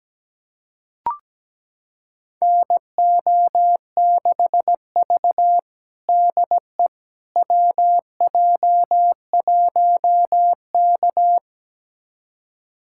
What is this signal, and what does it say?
Morse code, heard as a keyed tone, about 17 wpm, 700 Hz. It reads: NO6V DE WJ1K
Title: USA Callsign DE Callsign - 4 Characters - 17wpm